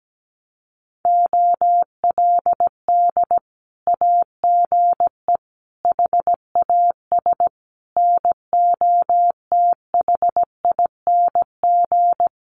Morse code, heard as a keyed tone, about 17 wpm, 700 Hz. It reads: OLD AGE HAS NOTHING
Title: Sentences from Top 1100 Words 17wpm